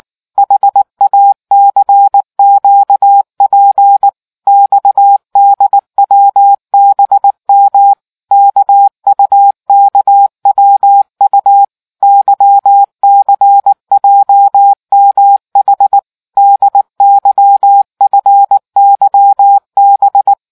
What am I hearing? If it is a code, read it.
Morse code, heard as a keyed tone, about 19 wpm, 800 Hz. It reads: HACQP XDWBM KUKWU YCJMH DYFYB